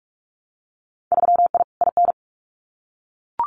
Morse code, heard as a keyed tone, about 46 wpm, 700 Hz. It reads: 3I IN
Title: Winter Field Day Exchanges 45wpm